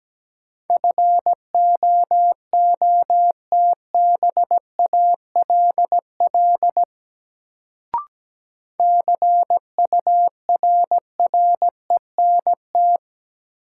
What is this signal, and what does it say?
Morse code, heard as a keyed tone, about 17 wpm, 700 Hz. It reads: FOOTBALL CURRENT
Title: Sentences from Top 1600 Words - New Word Drill Alt - 17wpm